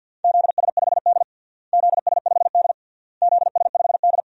Morse code, heard as a keyed tone, about 53 wpm, 700 Hz. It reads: ZS5D ZS5D ZS5D
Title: DXCC-100 Call Signs - 4 Characters - Repeated 3x - 50wpm